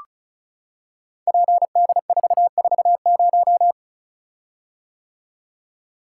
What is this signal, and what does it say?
Morse code, heard as a keyed tone, about 34 wpm, 700 Hz. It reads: PD440
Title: Code Groups - 5 Letter:Numbers - 35wpm